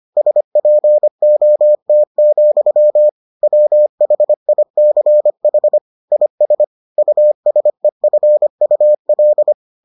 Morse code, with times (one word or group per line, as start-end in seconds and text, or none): 0.2-3.1 SPOT,
3.4-5.8 WHICH
6.1-6.6 IS
7.0-9.5 USEFUL